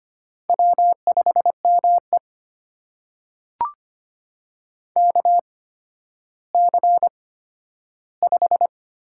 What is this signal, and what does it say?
Morse code, heard as a keyed tone, about 25 wpm, spaced as for 12 wpm, 700 Hz. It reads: W5ME KC5
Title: USA Call Signs - 8x Character Spacing - 25wpm